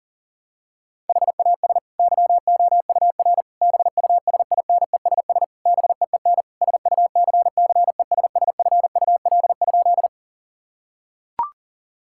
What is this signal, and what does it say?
Morse code, heard as a keyed tone, about 40 wpm, 700 Hz. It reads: HAS YOUR BUSINESS BEEN SUCCESSFUL?
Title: Sentences from Top 1000 Words - Review Part 6 - 40wpm